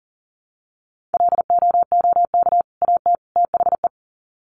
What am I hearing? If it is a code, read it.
Morse code, heard as a keyed tone, about 40 wpm, 700 Hz. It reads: LOOK AT THE
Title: Sets of 3 Words 40wpm